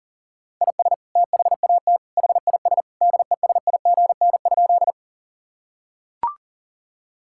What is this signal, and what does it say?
Morse code, heard as a keyed tone, about 40 wpm, 700 Hz. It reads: IS THAT HIS DESIGN?
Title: Sentences from Top 1000 Words - Review Part 5 - 40wpm